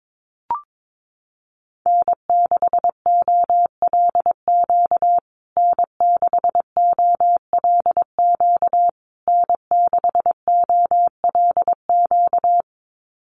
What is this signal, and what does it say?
Morse code, heard as a keyed tone, about 22 wpm, 700 Hz. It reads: N6OLQ N6OLQ N6OLQ
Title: USA Call Signs Repeated 3x 22wpm